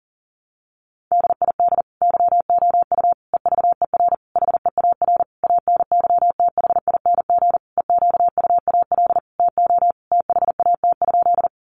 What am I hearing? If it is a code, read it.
Morse code, heard as a keyed tone, about 40 wpm, 700 Hz. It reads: DID YOU EVER HEAR ANYTHING EQUAL TO THAT?
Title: Sentences from Top 1400 Words 40wpm